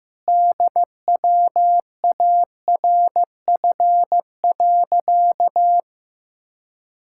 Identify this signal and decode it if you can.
Morse code, heard as a keyed tone, about 15 wpm, 700 Hz. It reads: DWARF.
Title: Pangram Challenge v2 - 15wpm